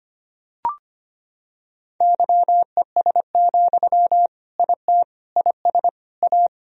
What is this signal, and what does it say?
Morse code, heard as a keyed tone, about 25 wpm, 700 Hz. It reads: YES, IT IS A
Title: Sentences from Top 1000 Words - Review Part 1 - 25wpm